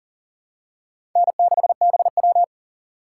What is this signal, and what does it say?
Morse code, heard as a keyed tone, about 40 wpm, 700 Hz. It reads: N6BW
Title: USA Call Signs - 6x Character Spacing - 40wpm